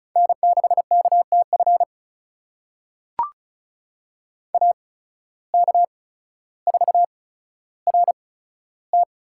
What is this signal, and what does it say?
Morse code, heard as a keyed tone, about 36 wpm, spaced as for 15 wpm, 690 Hz. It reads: N6KTF AK4RT
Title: USA Call Signs - 5 Characters - 8x Character Spacing - 35wpm